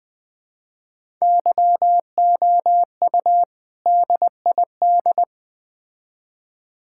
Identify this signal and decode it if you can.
Morse code, heard as a keyed tone, about 21 wpm, 700 Hz. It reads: YOU DID
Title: Build A Phrase - 3-Word Phrases - 20wpm